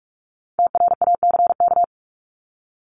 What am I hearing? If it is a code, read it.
Morse code, heard as a keyed tone, about 44 wpm, 700 Hz. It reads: TRACK
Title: Top 1001-1250 Words 45wpm